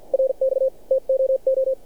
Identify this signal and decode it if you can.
Morse code, heard as a keyed tone, about 47 wpm, spaced as for 41 wpm, 530 Hz. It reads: RX TOO
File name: archived / CRSnet